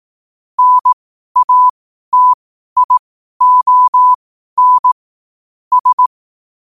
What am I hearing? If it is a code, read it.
Morse code, heard as a keyed tone, about 18 wpm, spaced as for 12 wpm, 1000 Hz. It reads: NATION S
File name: qotdfiles / 12wpm